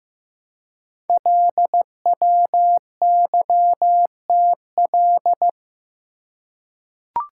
Morse code, heard as a keyed tone, about 15 wpm, 700 Hz. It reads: LWYTL
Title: Code Groups - 5 Letters - 15wpm